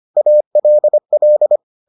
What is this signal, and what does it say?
Morse code, heard as a keyed tone, about 25 wpm, 600 Hz. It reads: ALL